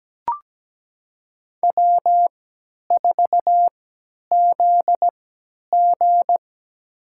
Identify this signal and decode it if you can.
Morse code, heard as a keyed tone, about 17 wpm, spaced as for 10 wpm, 700 Hz. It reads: W4ZG
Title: USA Call Signs - 4 Characters - 3x Character Spacing - 17wpm